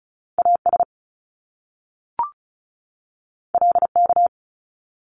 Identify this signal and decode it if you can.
Morse code, heard as a keyed tone, about 36 wpm, 700 Hz. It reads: AS LK
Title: Top Two Letter Combinations in Words 35wpm